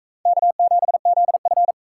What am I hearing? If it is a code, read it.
Morse code, heard as a keyed tone, about 44 wpm, 700 Hz. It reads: K7ZF